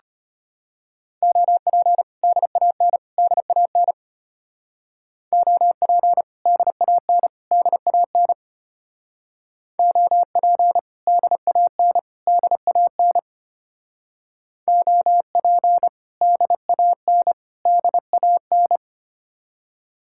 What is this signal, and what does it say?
Morse code, heard as a keyed tone, about 30 wpm, 700 Hz. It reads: OP DAN DAN OP DAN DAN OP DAN DAN OP DAN DAN
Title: Speed-Racing Common QSO Phrases - 25wpm